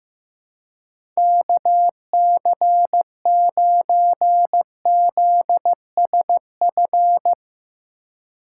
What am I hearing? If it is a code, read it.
Morse code, heard as a keyed tone, about 15 wpm, 700 Hz. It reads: KC9ZSF
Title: USA Call Signs - 6 Characters - 4x Character Spacing - 15wpm